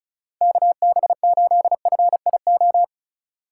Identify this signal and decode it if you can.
Morse code, heard as a keyed tone, about 34 wpm, 700 Hz. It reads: KB8FIO
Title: USA Call Signs - 6 Characters - 2x Character Spacing - 35wpm